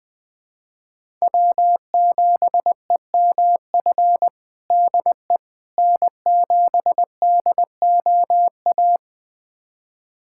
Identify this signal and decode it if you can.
Morse code, heard as a keyed tone, about 20 wpm, 700 Hz. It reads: W7EMF DE N7DOA
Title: USA Callsign DE Callsign - 5 Characters - 20wpm